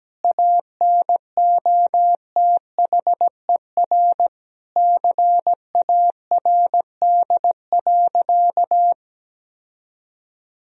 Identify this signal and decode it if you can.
Morse code, heard as a keyed tone, about 17 wpm, 700 Hz. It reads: ANOTHER CARD.
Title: Sentences from Top 1000 Words - Review Part 4 - 17wpm